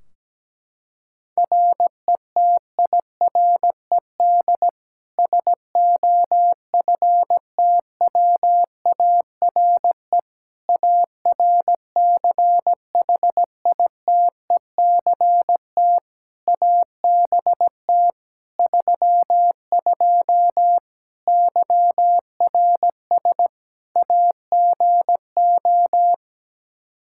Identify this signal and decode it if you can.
Morse code, heard as a keyed tone, about 17 wpm, 700 Hz. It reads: RETIRED SOFTWARE ARCHITECT ABT 32 YRS AGO
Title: Practice QSO - Exchange 3 - More Info - 17wpm